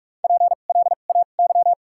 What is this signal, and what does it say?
Morse code, heard as a keyed tone, about 46 wpm, spaced as for 33 wpm, 700 Hz. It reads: PLAY
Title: 2x Character Spacing - Top 500 Words - 45wpm